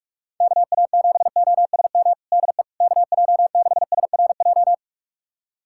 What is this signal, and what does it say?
Morse code, heard as a keyed tone, about 46 wpm, 690 Hz. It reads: KA7OSM DE KJ6SRJ
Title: USA Callsign DE Callsign 45wpm